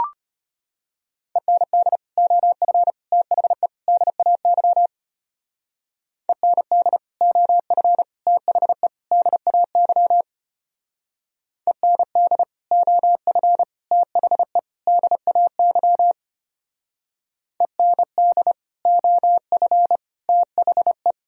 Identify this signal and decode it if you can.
Morse code, heard as a keyed tone, about 30 wpm, 700 Hz. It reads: END OF THE DAY END OF THE DAY END OF THE DAY END OF THE
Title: Speed-Racing - Sets of 4 Words 25wpm